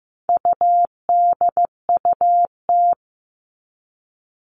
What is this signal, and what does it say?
Morse code, heard as a keyed tone, about 15 wpm, 700 Hz. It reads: UDUT
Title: All Cut Numbers - 4-Digit 15wpm